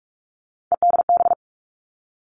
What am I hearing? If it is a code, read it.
Morse code, heard as a keyed tone, about 46 wpm, 700 Hz. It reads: EDB
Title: Three Challenging Letter-Number Combinations 45wpm